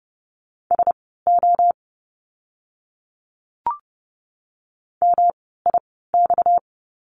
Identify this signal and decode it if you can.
Morse code, heard as a keyed tone, about 30 wpm, spaced as for 17 wpm, 700 Hz. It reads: SO MIX
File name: course / 30:10wpm / 230 Lesson 37f - Letter x - Words 30wpm